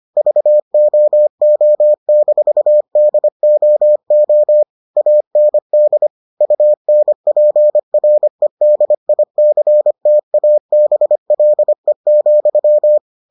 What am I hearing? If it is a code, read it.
Morse code, heard as a keyed tone, about 25 wpm, 600 Hz. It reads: VOO-DOO AND UNPREDICTABLE,